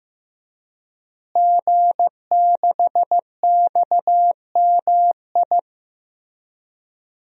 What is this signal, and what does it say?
Morse code, heard as a keyed tone, about 15 wpm, 700 Hz. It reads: G6XMI
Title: DXCC-100 Call Signs - 5 Characters - Repeated 2x - 15wpm